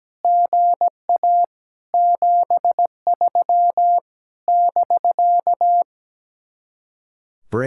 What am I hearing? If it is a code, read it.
Morse code, heard as a keyed tone, about 17 wpm, 700 Hz. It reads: GA 73 <BK>
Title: Speed-Racing Common QSO Phrases - 17wpm